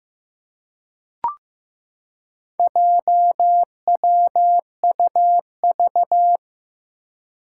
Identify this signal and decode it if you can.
Morse code, heard as a keyed tone, about 15 wpm, 700 Hz. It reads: JWUV
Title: Four Challenging Letter-Number Combinations 15wpm